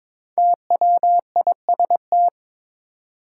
Morse code, heard as a keyed tone, about 22 wpm, 700 Hz. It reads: TWIST
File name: 5 Letter Words - First 3 Letters Given - 22wpm